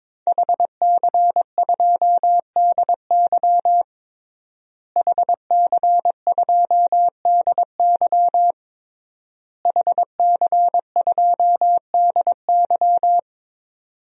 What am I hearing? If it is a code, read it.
Morse code, heard as a keyed tone, about 23 wpm, 700 Hz. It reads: HC2DY HC2DY HC2DY
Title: DXCC-100 Call Signs - 5 Characters - Repeated 3x - 22wpm